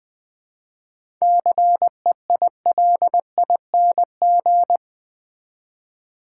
Morse code, heard as a keyed tone, about 21 wpm, 690 Hz. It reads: CEILING
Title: 7 Letter Words - First 3 Letters Given - 20wpm